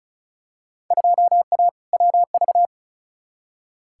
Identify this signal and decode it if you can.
Morse code, heard as a keyed tone, about 36 wpm, 690 Hz. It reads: 2A WV
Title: Field Day Exchanges Encore - 35wpm